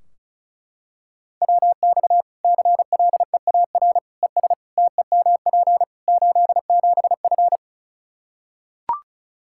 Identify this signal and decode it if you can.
Morse code, heard as a keyed tone, about 36 wpm, 700 Hz. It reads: WX CLEAR ES TEMP 87F
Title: Practice QSO - Exchange 2 - Info - 35wpm